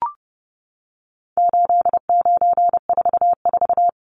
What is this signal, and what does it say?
Morse code, heard as a keyed tone, about 30 wpm, 700 Hz. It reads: 8944
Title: Numbers - 4-Digit 30wpm